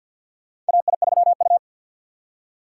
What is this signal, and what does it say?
Morse code, heard as a keyed tone, about 52 wpm, 700 Hz. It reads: AI3U
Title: USA Call Signs - 2x Character Spacing - 50wpm